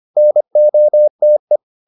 Morse code, heard as a keyed tone, about 25 wpm, 600 Hz. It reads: NOTE